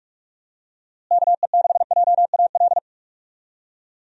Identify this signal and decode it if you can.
Morse code, heard as a keyed tone, about 46 wpm, 700 Hz. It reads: KE6JAL
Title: USA Call Signs - 6 Characters - Repeated 2x - 45wpm